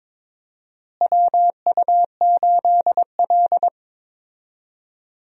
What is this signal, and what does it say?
Morse code, heard as a keyed tone, about 22 wpm, 700 Hz. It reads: WU8L
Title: USA Call Signs - 4 Characters - Repeated 3x - 22wpm